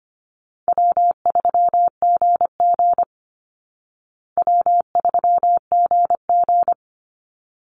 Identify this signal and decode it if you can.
Morse code, heard as a keyed tone, about 25 wpm, 700 Hz. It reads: W3GG W3GG
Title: Call Signs Repeated 3x - 25wpm